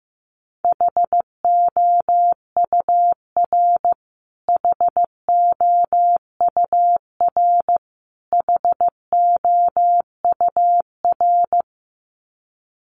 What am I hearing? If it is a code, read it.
Morse code, heard as a keyed tone, about 15 wpm, 700 Hz. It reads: HOUR HOUR HOUR
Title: Top 500 Words Repeated 15wpm